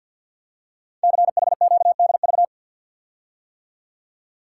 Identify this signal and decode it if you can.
Morse code, heard as a keyed tone, about 51 wpm, 700 Hz. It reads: CHQDV
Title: Code Groups - 5 Letter:Numbers - 50wpm